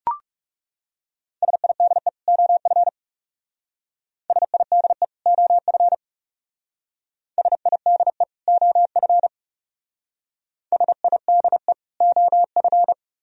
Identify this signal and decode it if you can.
Morse code, heard as a keyed tone, about 36 wpm, 700 Hz. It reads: SIDE OF SIDE OF SIDE OF SIDE OF
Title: Speed-Racing - Sets of 2 Words 30wpm